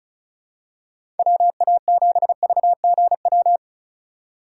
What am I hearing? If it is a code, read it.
Morse code, heard as a keyed tone, about 34 wpm, 700 Hz. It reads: WA7VGW
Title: USA Call Signs - 6 Characters - 4x Character Spacing - 35wpm